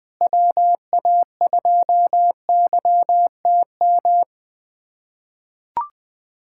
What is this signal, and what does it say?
Morse code, heard as a keyed tone, about 21 wpm, 700 Hz. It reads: WA2YTM
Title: USA Callsign DE Callsign - 6 Characters - 20wpm